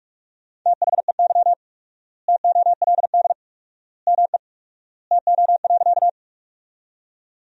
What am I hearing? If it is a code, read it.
Morse code, heard as a keyed tone, about 46 wpm, 700 Hz. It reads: THEY TOLD ME TO.
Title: 4x Word-Spacing - Sentences from Top 400 Words - 45wpm